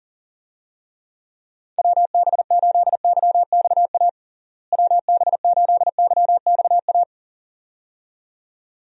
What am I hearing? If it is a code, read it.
Morse code, heard as a keyed tone, about 40 wpm, 700 Hz. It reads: WB8YXA WB8YXA
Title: USA Call Signs - 6 Characters - Repeated 2x - 40wpm